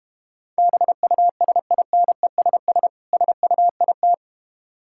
Morse code, heard as a keyed tone, about 32 wpm, 700 Hz. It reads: BUSINESS SUIT